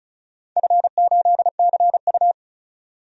Morse code, heard as a keyed tone, about 36 wpm, 690 Hz. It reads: F8CU
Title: DXCC-100 Call Signs - 4 Characters - 35wpm